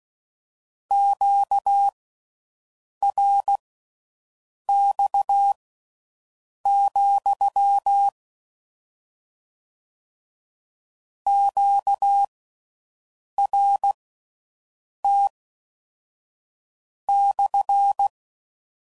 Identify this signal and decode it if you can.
Morse code, heard as a keyed tone, about 16 wpm, spaced as for 6 wpm, 800 Hz. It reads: QRX, QRT /